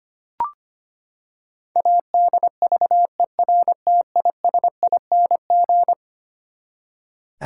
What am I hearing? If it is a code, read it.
Morse code, heard as a keyed tone, about 25 wpm, 700 Hz. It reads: ADVERTISING
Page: morse-code-ninja-practice-25wpm